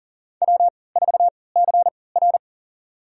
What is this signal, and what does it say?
Morse code, heard as a keyed tone, about 40 wpm, spaced as for 23 wpm, 700 Hz. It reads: W4CR
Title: USA Call Signs - 4 Characters - 3x Character Spacing - 40wpm